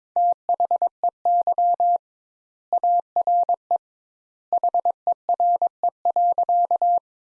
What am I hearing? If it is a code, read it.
Morse code, heard as a keyed tone, about 22 wpm, 700 Hz. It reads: THEY ARE HERE.
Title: Sentences Top 200 Words - 22wpm